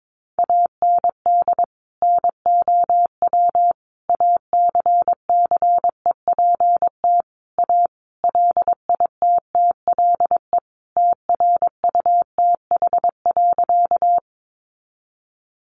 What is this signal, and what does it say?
Morse code, heard as a keyed tone, about 22 wpm, 700 Hz. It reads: AND NOW ACCEPT A LITTLE TRUTH.
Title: Sentences from Top 1300 Words 22wpm